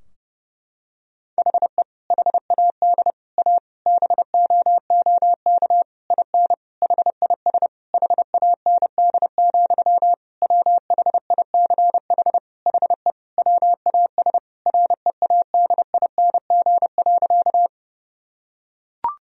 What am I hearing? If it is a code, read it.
Morse code, heard as a keyed tone, about 30 wpm, 700 Hz. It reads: HE HAD A BOOK IN HIS HAND, WHICH HE WAS READING.
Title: Sentences from Top 1000 Words - Review Part 3 - 30wpm